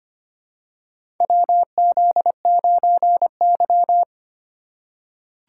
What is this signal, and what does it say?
Morse code, heard as a keyed tone, about 25 wpm, 700 Hz. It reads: WZ9Y
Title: USA Call Signs - 4 Characters - 4x Character Spacing - 25wpm